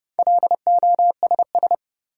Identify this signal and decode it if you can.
Morse code, heard as a keyed tone, about 30 wpm, 700 Hz. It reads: LOSS
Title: Sentences from Top 1400 Words - New Word Drill - 30wpm